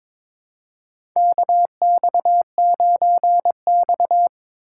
Morse code, heard as a keyed tone, about 23 wpm, 700 Hz. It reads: KX9X
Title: USA Call Signs - 4 Characters - 3x Character Spacing - 22wpm